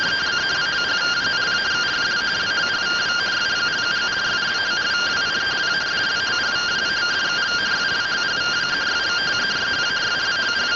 Two-tone teleprinter signal, about 170 Hz shift.